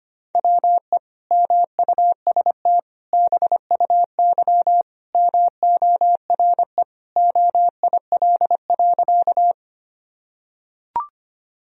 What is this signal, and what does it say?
Morse code, heard as a keyed tone, about 25 wpm, 700 Hz. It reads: WE MUST BUY MORE OIL.
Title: Sentences from Top 1000 Words - Review Part 4 - 25wpm